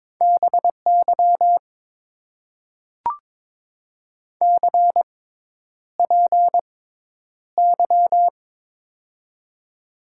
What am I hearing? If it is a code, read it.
Morse code, heard as a keyed tone, about 22 wpm, spaced as for 10 wpm, 700 Hz. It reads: BY CPY
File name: Top 90 QSO Elements - 6x Character Spacing - 22wpm